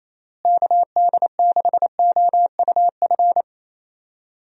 Morse code, heard as a keyed tone, about 27 wpm, 700 Hz. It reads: KD6OUF
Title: USA Call Signs - 6 Characters - Repeated 3x - 28wpm